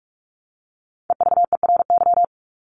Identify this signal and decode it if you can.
Morse code, heard as a keyed tone, about 44 wpm, 700 Hz. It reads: EVERY
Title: Top 200 Words Repeated 45wpm